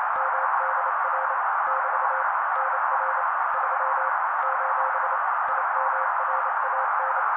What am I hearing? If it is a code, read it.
Morse code, heard as a keyed tone, about 28 wpm, 530 Hz. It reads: MDR <BT> NR 3 : IMRAN
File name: daily / 27wpm noise